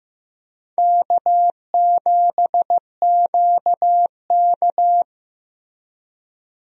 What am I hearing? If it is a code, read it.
Morse code, heard as a keyed tone, about 15 wpm, 700 Hz. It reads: K7QK
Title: USA Call Signs - 4 Characters - Repeated 2x - 15wpm